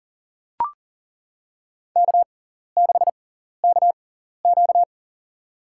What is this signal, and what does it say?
Morse code, heard as a keyed tone, about 40 wpm, spaced as for 14 wpm, 690 Hz. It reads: K6KQ
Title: USA Call Signs - 4 Characters - 6x Character Spacing - 40wpm